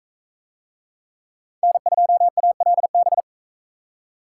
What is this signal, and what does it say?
Morse code, heard as a keyed tone, about 43 wpm, 700 Hz. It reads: N2ALB